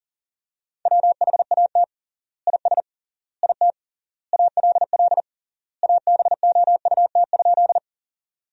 Morse code, heard as a keyed tone, about 38 wpm, 700 Hz. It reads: WHAT IS IT ALL ABOUT?
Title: 3x Word-Spacing - Sentences from Top 100 Words - 40wpm